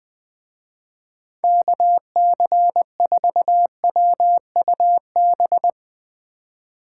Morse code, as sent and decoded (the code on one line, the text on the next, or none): -.- -.-. ....- .-- ..- -...
KC4WUB